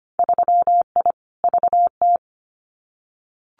Morse code, heard as a keyed tone, about 25 wpm, 700 Hz. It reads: 3I VT
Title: Winter Field Day Exchanges Encore 25wpm